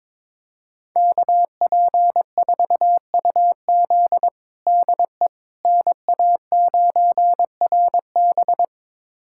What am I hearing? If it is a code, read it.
Morse code, heard as a keyed tone, about 22 wpm, 700 Hz. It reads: KP4UZ DE NA9RB
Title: USA Callsign DE Callsign - 5 Characters - 22wpm